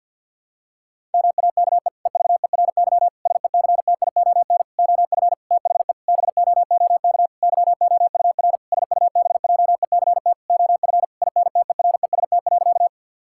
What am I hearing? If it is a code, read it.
Morse code, heard as a keyed tone, about 50 wpm, 700 Hz. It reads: MAKE EVERY SECTION OF THE BOOK YOUR SUBJECT OF INTEREST.
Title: Sentences from Top 1000 Words - Review Part 5 - 50wpm